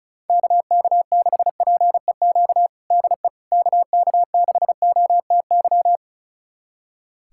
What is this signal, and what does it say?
Morse code, heard as a keyed tone, about 36 wpm, 700 Hz. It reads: KK6PEQ DE KK6OTY
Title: USA Callsign DE Callsign - 6 Characters - 35wpm